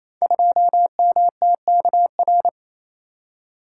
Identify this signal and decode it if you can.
Morse code, heard as a keyed tone, about 27 wpm, 690 Hz. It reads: 2MTKR
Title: Code Groups - 5 Letter:Numbers - 28wpm